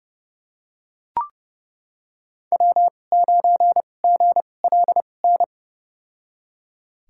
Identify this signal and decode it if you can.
Morse code, heard as a keyed tone, about 30 wpm, spaced as for 22 wpm, 700 Hz. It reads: W9GLN
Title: USA Call Signs - 5 Characters - 2x Character Spacing - 30wpm